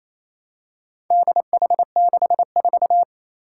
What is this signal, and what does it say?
Morse code, heard as a keyed tone, about 28 wpm, 700 Hz. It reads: DH64
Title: Four Challenging Letter-Number Combinations 28wpm